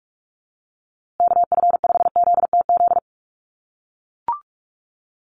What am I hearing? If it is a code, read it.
Morse code, heard as a keyed tone, about 44 wpm, 700 Hz. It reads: KF5ZTZ
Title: USA Call Signs Repeated 3x 45wpm